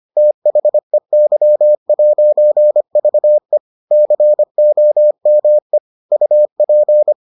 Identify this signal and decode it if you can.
Morse code, heard as a keyed tone, about 25 wpm, 600 Hz. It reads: THEY'VE COME UP